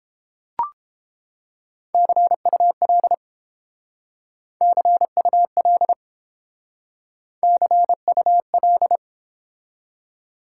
Morse code, heard as a keyed tone, about 30 wpm, 700 Hz. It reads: CUL CUL CUL